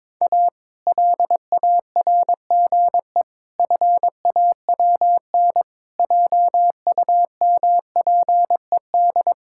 Morse code, heard as a keyed tone, about 22 wpm, 700 Hz. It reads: A LARGE FAWN JUMPED
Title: Pangram Challenge v2 - 22wpm